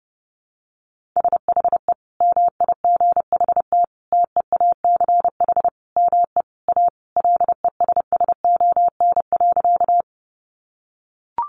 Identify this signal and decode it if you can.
Morse code, heard as a keyed tone, about 28 wpm, 700 Hz. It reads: SHE MIGHT TEACH ME A LESSON.
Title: Sentences from Top 1200 Words 30wpm